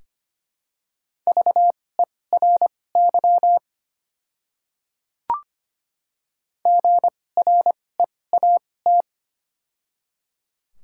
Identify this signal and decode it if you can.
Morse code, heard as a keyed tone, about 25 wpm, spaced as for 18 wpm, 690 Hz. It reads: VERY GREAT
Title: 2x Character Spacing - Top 200 Words - 25wpm